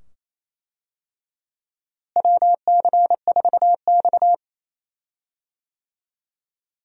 Morse code, heard as a keyed tone, about 27 wpm, 700 Hz. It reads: WC4X
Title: USA Call Signs - 4 Characters - Repeated 2x - 28wpm